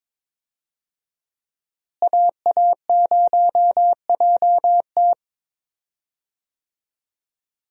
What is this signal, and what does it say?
Morse code, heard as a keyed tone, about 22 wpm, 700 Hz. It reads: AA0JT
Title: USA Call Signs - 5 Characters - Repeated 2x - 22wpm